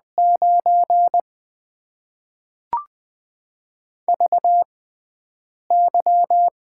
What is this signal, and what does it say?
Morse code, heard as a keyed tone, about 20 wpm, spaced as for 7 wpm, 700 Hz. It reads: 9 VY
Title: Top 90 QSO Elements - 6x Character Spacing - 20wpm